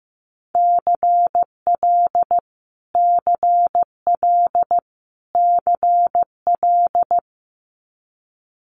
Morse code, heard as a keyed tone, about 16 wpm, 700 Hz. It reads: CL CL CL
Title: Multi-letter Phonemes Repeated 3x 15wpm